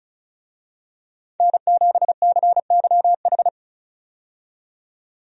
Morse code, heard as a keyed tone, about 36 wpm, 700 Hz. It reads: N7CYH
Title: USA Call Signs - 5 Characters - 35wpm